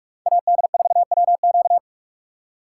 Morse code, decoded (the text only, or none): AD4WQ